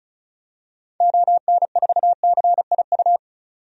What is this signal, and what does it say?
Morse code, heard as a keyed tone, about 36 wpm, 700 Hz. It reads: ON4CIU
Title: DXCC-100 Call Signs - 6 Characters - 35wpm